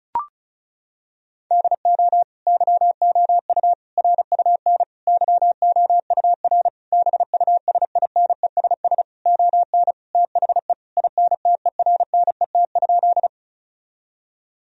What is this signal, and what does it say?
Morse code, heard as a keyed tone, about 34 wpm, 700 Hz. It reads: DO YOU RUN YOUR BUSINESS ON THE INTERNET?
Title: Sentences from Top 1000 Words - Review Part 4 - 35wpm